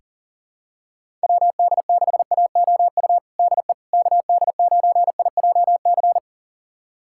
Morse code, heard as a keyed tone, about 40 wpm, 700 Hz. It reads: WD6AOU DE KD9IJC
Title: USA Callsign DE Callsign - 6 Characters - 40wpm